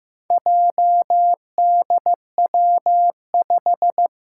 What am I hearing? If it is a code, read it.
Morse code, heard as a keyed tone, about 15 wpm, 700 Hz. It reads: JDW5
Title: Four Challenging Letter-Number Combinations 15wpm